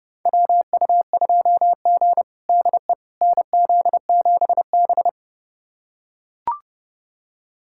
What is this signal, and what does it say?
Morse code, heard as a keyed tone, about 30 wpm, 700 Hz. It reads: WU2G DE NZ7B
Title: USA Callsign DE Callsign - 4 Characters - 30wpm